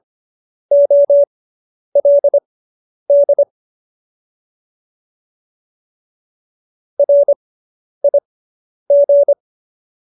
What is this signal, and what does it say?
Morse code, heard as a keyed tone, about 25 wpm, spaced as for 10 wpm, 570 Hz. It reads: OLD RIG